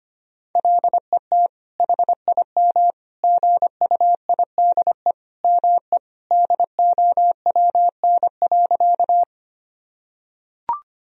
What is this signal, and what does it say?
Morse code, heard as a keyed tone, about 25 wpm, 700 Hz. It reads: LET HIM GUIDE ME DOWN.
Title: Sentences from Top 1000 Words - Review Part 3 - 25wpm